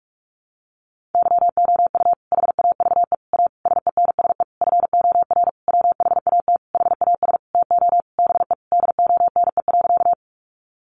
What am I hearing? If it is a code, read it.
Morse code, heard as a keyed tone, about 44 wpm, 700 Hz. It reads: YOU HAVE A SENSE FOR WHAT HAS TO BE DONE.
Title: Sentences from Top 700 Words 45wpm